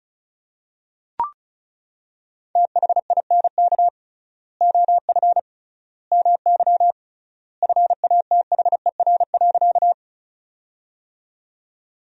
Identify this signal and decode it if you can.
Morse code, heard as a keyed tone, about 36 wpm, 700 Hz. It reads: THINK OF MY FATHER.
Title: 3x Word-Spacing - Sentences from Top 300 Words - 35wpm